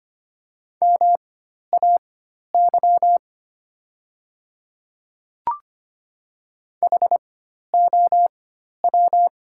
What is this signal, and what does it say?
Morse code, heard as a keyed tone, about 25 wpm, spaced as for 12 wpm, 700 Hz. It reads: MAY HOW
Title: 4x Character Spacing - Top 100 Words - 25wpm